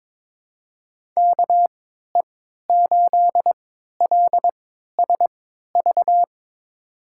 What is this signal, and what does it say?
Morse code, heard as a keyed tone, about 23 wpm, spaced as for 13 wpm, 700 Hz. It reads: KE8LSV
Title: USA Call Signs - 6 Characters - 3x Character Spacing - 22wpm